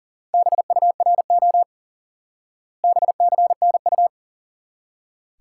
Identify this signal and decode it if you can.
Morse code, heard as a keyed tone, about 37 wpm, 700 Hz. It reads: BURO BCNU
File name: IQE Course 3 - Lesson 89 - YAGI - Two Elements - 40wpm